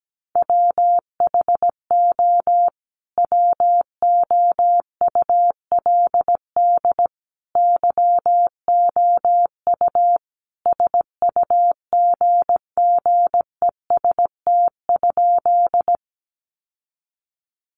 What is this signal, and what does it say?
Morse code, heard as a keyed tone, about 17 wpm, 700 Hz. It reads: WHO WOULD YOU SUGGEST?
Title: Sentences from Top 1300 Words 17wpm